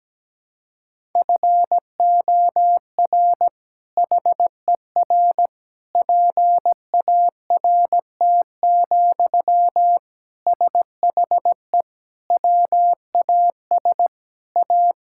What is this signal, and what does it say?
Morse code, heard as a keyed tone, about 17 wpm, 700 Hz. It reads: FOR HER PART, SHE WAS A
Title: Sentences from Top 1000 Words - Review Part 5 - 17wpm